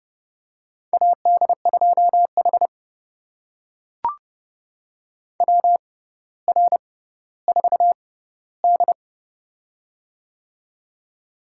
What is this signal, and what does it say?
Morse code, heard as a keyed tone, about 29 wpm, spaced as for 15 wpm, 700 Hz. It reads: AD2H WR4D
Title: USA Call Signs - 4 Characters - 6x Character Spacing - 30wpm